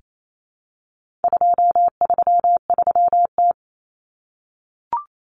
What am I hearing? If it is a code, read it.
Morse code, heard as a keyed tone, about 28 wpm, 700 Hz. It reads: 233T
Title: Common Cut Numbers - 4-Digit 28wpm